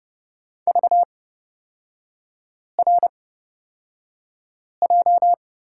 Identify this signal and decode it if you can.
Morse code, heard as a keyed tone, about 30 wpm, spaced as for 5 wpm, 700 Hz. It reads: VRJ
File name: Lesson 37e - Letter x - Three Characters 30wpm